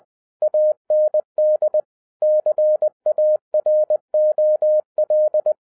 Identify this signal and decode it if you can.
Morse code, heard as a keyed tone, about 20 wpm, 610 Hz. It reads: AND CAROL